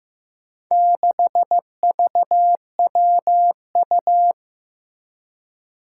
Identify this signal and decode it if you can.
Morse code, heard as a keyed tone, about 15 wpm, 700 Hz. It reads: 6VWU
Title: Four Challenging Letter-Number Combinations 15wpm